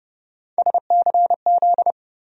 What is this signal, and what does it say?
Morse code, heard as a keyed tone, about 30 wpm, 700 Hz. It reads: SCZ
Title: USA Call Sign Postfixes - 3 Letters 30wpm